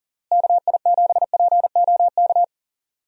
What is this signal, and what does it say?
Morse code, heard as a keyed tone, about 40 wpm, 700 Hz. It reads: KI7POK
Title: USA Call Signs - 6 Characters - Repeated 3x - 40wpm